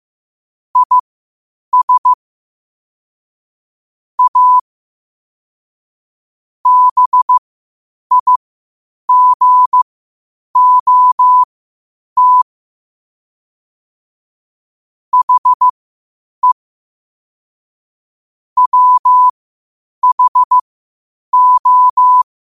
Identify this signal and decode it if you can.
Morse code, heard as a keyed tone, about 15 wpm, spaced as for 8 wpm, 1000 Hz. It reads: IS A BIGOT HE WHO
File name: qotdfiles / 8wpm